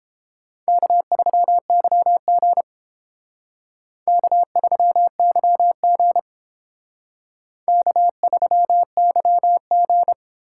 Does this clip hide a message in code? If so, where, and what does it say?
Yes: Morse code from 0.7 to 10.1 s, heard as a keyed tone, about 30 wpm, 700 Hz. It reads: K3YG K3YG K3YG